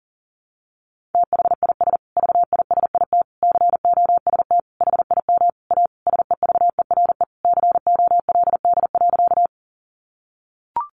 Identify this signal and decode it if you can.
Morse code, heard as a keyed tone, about 40 wpm, 700 Hz. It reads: THIS VISIT COST HIM A SEVERE COLD.
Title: Sentences from Top 1400 Words 40wpm